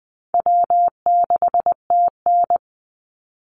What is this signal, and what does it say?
Morse code, heard as a keyed tone, about 20 wpm, 700 Hz. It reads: W6TN
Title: Easy Call Signs Repeated 2x - 20wpm